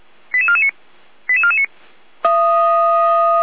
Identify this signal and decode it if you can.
Five-tone selective call (ZVEI), heard as quick runs of five tones, tones of about 70 ms. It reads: 88449 88449
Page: /455/SDR/ZVEI/